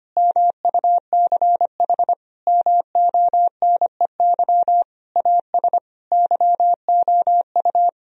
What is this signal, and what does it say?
Morse code, heard as a keyed tone, about 25 wpm, 700 Hz. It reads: MUCH MONEY AS YOU
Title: Sentences from Top 1000 Words - Review Part 3 - 25wpm